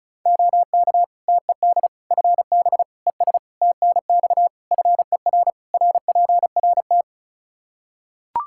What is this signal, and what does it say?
Morse code, heard as a keyed tone, about 34 wpm, 700 Hz. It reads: OK TED FB ES TNX FER RPRT
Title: Practice QSO - Exchange 2 - Info - 35wpm